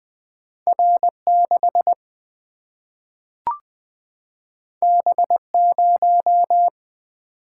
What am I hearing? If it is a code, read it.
Morse code, heard as a keyed tone, about 20 wpm, 700 Hz. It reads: R6 B0
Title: Single-Letter Single-Number 20wpm